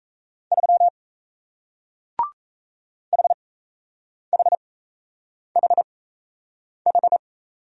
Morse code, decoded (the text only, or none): 3 HHHH